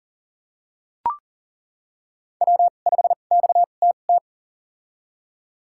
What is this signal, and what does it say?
Morse code, heard as a keyed tone, about 40 wpm, spaced as for 29 wpm, 700 Hz. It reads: W5XTT
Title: USA Call Signs - 5 Characters - 2x Character Spacing - 40wpm